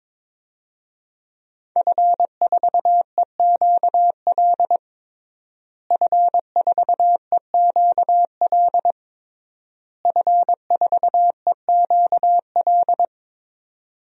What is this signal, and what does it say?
Morse code, heard as a keyed tone, about 21 wpm, 700 Hz. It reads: F4EQL F4EQL F4EQL
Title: DXCC-100 Call Signs - 5 Characters - Repeated 3x - 22wpm